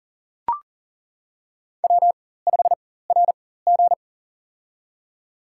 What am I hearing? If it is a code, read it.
Morse code, heard as a keyed tone, about 37 wpm, spaced as for 18 wpm, 700 Hz. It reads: W5RG
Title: USA Call Signs - 4 Characters - 4x Character Spacing - 40wpm